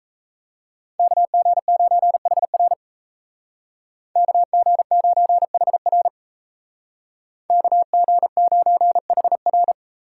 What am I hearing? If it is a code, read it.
Morse code, heard as a keyed tone, about 33 wpm, 700 Hz. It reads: KG9HR KG9HR KG9HR
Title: Speed Racing - USA 2x2 Call Signs 28wpm